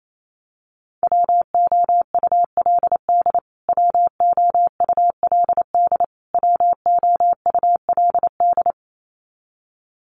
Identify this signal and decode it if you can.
Morse code, heard as a keyed tone, about 28 wpm, 700 Hz. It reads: WOULD WOULD WOULD
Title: Top 500 Words Repeated 28wpm